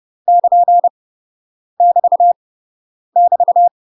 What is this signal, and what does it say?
Morse code, heard as a keyed tone, about 30 wpm, 700 Hz. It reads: <KN> <BT> <BT>
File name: UK MASTER QSOs TRAINING / 30WPM / UK MASTER QSOs 0020